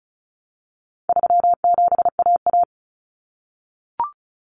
Morse code, heard as a keyed tone, about 34 wpm, 700 Hz. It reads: 37AA
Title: Common Cut Numbers - 4-Digit 35wpm